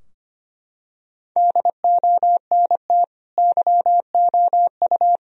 Morse code, heard as a keyed tone, about 25 wpm, 700 Hz. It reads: DONT YOU
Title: Sentences from Top 1000 Words - Review Part 2 - 25wpm